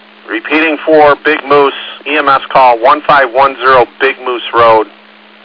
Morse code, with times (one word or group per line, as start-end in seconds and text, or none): none